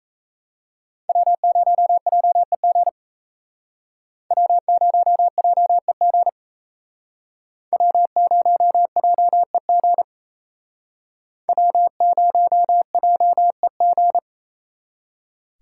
Morse code, heard as a keyed tone, about 35 wpm, 700 Hz. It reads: W0JEG W0JEG W0JEG W0JEG
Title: Speed Racing - USA 1x3 Call Signs 28wpm